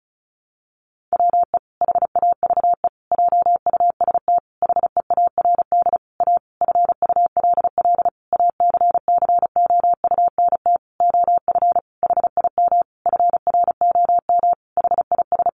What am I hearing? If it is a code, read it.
Morse code, heard as a keyed tone, about 34 wpm, 700 Hz. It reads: WE HAVE JUST HEARD A FULL ACCOUNT OF HIM FROM HIS
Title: Sentences from Top 900 Words 35wpm